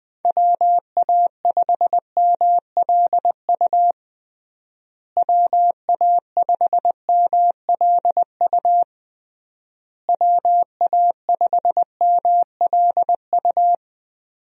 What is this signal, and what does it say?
Morse code, heard as a keyed tone, about 21 wpm, 700 Hz. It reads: WA5MLU WA5MLU WA5MLU
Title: USA Call Signs - 6 Characters - Repeated 3x - 20wpm